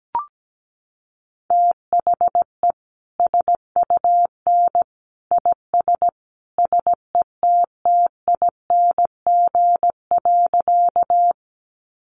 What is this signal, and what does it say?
Morse code, heard as a keyed tone, about 17 wpm, 700 Hz. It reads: THE SUN IS SETTING.
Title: Sentences from Top 1300 Words 17wpm